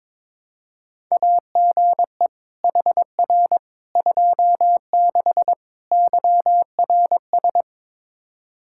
Morse code, heard as a keyed tone, about 22 wpm, 700 Hz. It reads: AGE HR 26 YRS
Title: Practice QSO - Exchange 3 - More Info - 22wpm